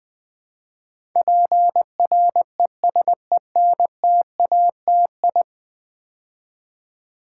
Morse code, heard as a keyed tone, about 20 wpm, 700 Hz. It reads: PRESENTATI
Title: Build A Word - 5 Letters At A Time - 12-Letter Words - 20wpm